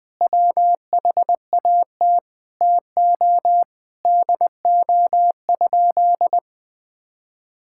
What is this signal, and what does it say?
Morse code, heard as a keyed tone, about 19 wpm, 700 Hz. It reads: WHAT TO DO?
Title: Sentences with 2-Word Phrases - Spoken Beginning - 20wpm